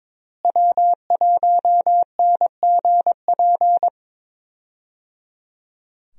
Morse code, heard as a keyed tone, about 22 wpm, 700 Hz. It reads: W1NGP
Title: USA Call Signs - 5 Characters - Repeated 3x - 22wpm